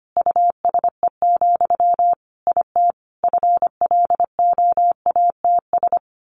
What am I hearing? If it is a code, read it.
Morse code, heard as a keyed tone, about 25 wpm, 700 Hz. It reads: USE, IT FLOATS
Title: A Sound Of Thunder 25wpm